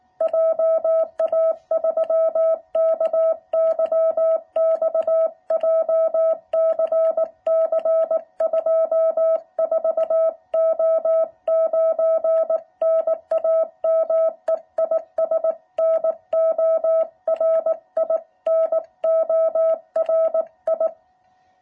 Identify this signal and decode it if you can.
Morse code, heard as a keyed tone, about 18 wpm, 660 Hz. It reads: JA3KYX JCC 24O9 NAME IS NORI NORI